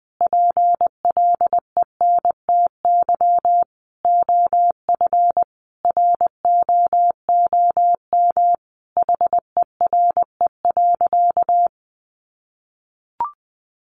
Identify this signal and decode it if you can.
Morse code, heard as a keyed tone, about 20 wpm, 700 Hz. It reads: PLENTY OF ROOM HERE.
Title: Sentences from Top 1300 Words 20wpm